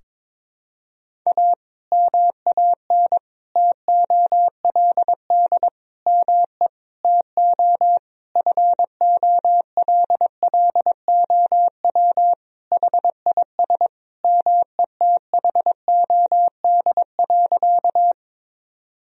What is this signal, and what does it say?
Morse code, heard as a keyed tone, about 22 wpm, 700 Hz. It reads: A MAN TOLD ME TO FOLLOW HIS METHOD.
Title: Sentences from Top 1000 Words - Review Part 3 - 22wpm